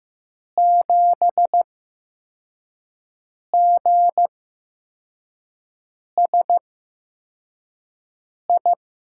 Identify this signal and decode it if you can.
Morse code, heard as a keyed tone, about 15 wpm, spaced as for 4 wpm, 700 Hz. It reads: 7GSI
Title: USA Call Signs - 5 Characters - 8x Character Spacing - 15wpm